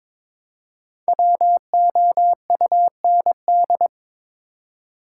Morse code, heard as a keyed tone, about 22 wpm, 700 Hz. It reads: WOUND
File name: 5 Letter Words - First 3 Letters Given - 22wpm